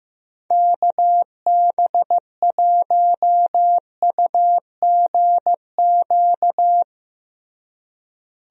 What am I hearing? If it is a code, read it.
Morse code, heard as a keyed tone, about 15 wpm, 700 Hz. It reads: KB1UGQ
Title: USA Call Signs - 6 Characters - Repeated 3x - 15wpm